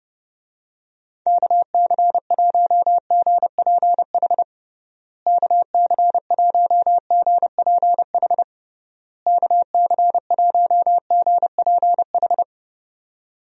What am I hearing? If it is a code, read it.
Morse code, heard as a keyed tone, about 32 wpm, 700 Hz. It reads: KC1GPH KC1GPH KC1GPH
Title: USA Call Signs - 6 Characters - Repeated 3x - 30wpm